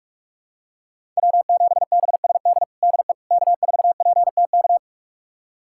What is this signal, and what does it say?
Morse code, heard as a keyed tone, about 46 wpm, 700 Hz. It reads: W7BSD DE K4PTK
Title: USA Callsign DE Callsign - 5 Characters - 45wpm